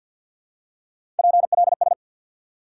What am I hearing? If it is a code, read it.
Morse code, heard as a keyed tone, about 50 wpm, 700 Hz. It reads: PLS
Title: Top 90 QSO Elements 50wpm